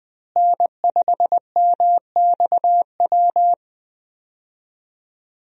Morse code, heard as a keyed tone, about 21 wpm, 700 Hz. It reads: N5MXW
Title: USA Call Signs - 5 Characters - Repeated 3x - 20wpm